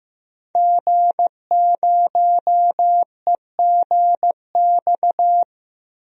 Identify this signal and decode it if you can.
Morse code, heard as a keyed tone, about 15 wpm, 700 Hz. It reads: G0EGX
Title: DXCC-100 Call Signs - 5 Characters - 15wpm